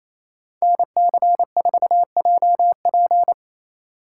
Morse code, heard as a keyed tone, about 33 wpm, 700 Hz. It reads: NC4JP
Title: Speed Racing - USA 2x2 Call Signs 28wpm